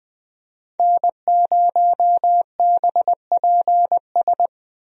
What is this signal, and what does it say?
Morse code, heard as a keyed tone, about 20 wpm, 700 Hz. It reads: N0BPS
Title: USA Call Signs - 5 Characters - Repeated 2x - 20wpm